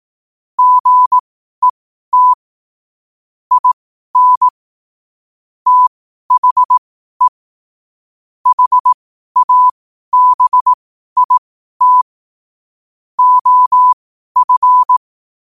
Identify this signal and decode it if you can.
Morse code, heard as a keyed tone, about 18 wpm, spaced as for 12 wpm, 1000 Hz. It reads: GET IN THE HABIT OF
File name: qotdfiles / 12wpm